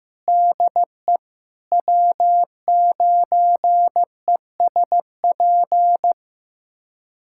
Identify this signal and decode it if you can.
Morse code, heard as a keyed tone, about 15 wpm, 700 Hz. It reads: DE W9ESP
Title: USA Callsign DE Callsign 15wpm